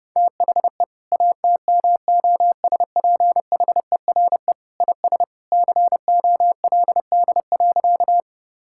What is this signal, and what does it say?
Morse code, heard as a keyed tone, about 32 wpm, 700 Hz. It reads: THE ATMOSPHERE IS COLD.
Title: Sentences from Top 1600 Words 30wpm